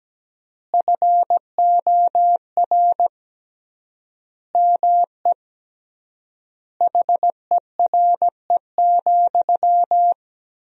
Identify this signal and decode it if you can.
Morse code, heard as a keyed tone, about 17 wpm, 700 Hz. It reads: FOR ME HERE,
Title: 3x Word-Spacing - Sentences from Top 500 Words - 17wpm